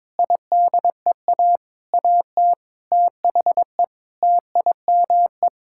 Morse code, heard as a keyed tone, about 22 wpm, 700 Hz. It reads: IDEA AT THE TIME
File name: practice / 22wpm / Sets of 8 Words - 22wpm